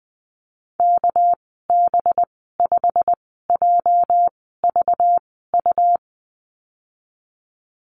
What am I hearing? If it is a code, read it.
Morse code, heard as a keyed tone, about 20 wpm, spaced as for 14 wpm, 700 Hz. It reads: KB5JVU
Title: 245 Lesson 39g - Letter z - Call Signs 20wpm